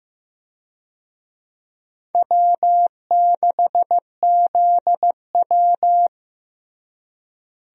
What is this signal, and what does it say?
Morse code, heard as a keyed tone, about 15 wpm, 700 Hz. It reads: W6ZW
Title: USA Call Signs - 4 Characters - Repeated 2x - 15wpm